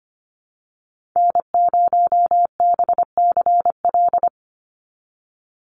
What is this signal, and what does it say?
Morse code, heard as a keyed tone, about 24 wpm, 700 Hz. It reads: N0BCL
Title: USA Call Signs 25wpm